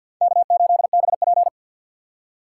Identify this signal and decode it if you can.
Morse code, heard as a keyed tone, about 51 wpm, 700 Hz. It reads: K8BP
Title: USA Call Signs - 4 Characters - 6x Character Spacing - 50wpm